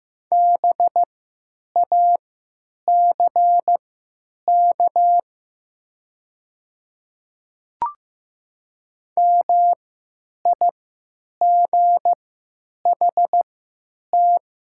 Morse code, heard as a keyed tone, about 15 wpm, spaced as for 9 wpm, 700 Hz. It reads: BACK MIGHT
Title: 3x Character Spacing - Top 500 Words - 15wpm